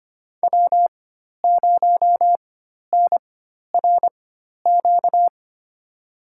Morse code, heard as a keyed tone, about 25 wpm, spaced as for 12 wpm, 700 Hz. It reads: W0NRQ
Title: USA Call Signs - 5 Characters - 4x Character Spacing - 25wpm